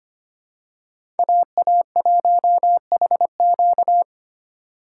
Morse code, heard as a keyed tone, about 25 wpm, 700 Hz. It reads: AA1HQ
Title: USA Call Signs - 5 Characters - 6x Character Spacing - 25wpm